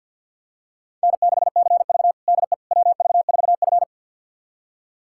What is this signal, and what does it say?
Morse code, heard as a keyed tone, about 53 wpm, 700 Hz. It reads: N6CV DE WV4F
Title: USA Callsign DE Callsign - 4 Characters - 50wpm